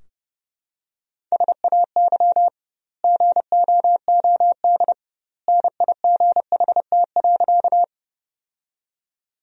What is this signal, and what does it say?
Morse code, heard as a keyed tone, about 32 wpm, 700 Hz. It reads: SAY GOOD NIGHT.
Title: 2x Word-Spacing - Sentences from Top 300 Words - 30wpm